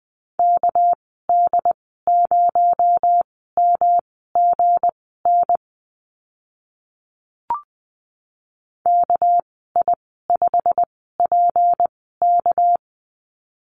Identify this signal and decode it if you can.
Morse code, heard as a keyed tone, about 20 wpm, spaced as for 14 wpm, 700 Hz. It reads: KD0MGN KI5PK
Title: 245 Lesson 39g - Letter z - Call Signs 20wpm